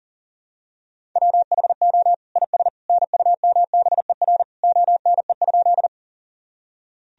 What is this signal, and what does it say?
Morse code, heard as a keyed tone, about 40 wpm, 700 Hz. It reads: WHO IS NUMBER ONE?
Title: Sentences from Top 1000 Words - Review Part 1 - 40wpm